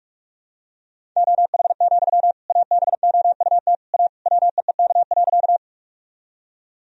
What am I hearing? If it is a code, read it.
Morse code, heard as a keyed tone, about 44 wpm, 700 Hz. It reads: OH, ABOUT A WEEK.
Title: Sentences from Top 1000 Words - Review Part 2 - 45wpm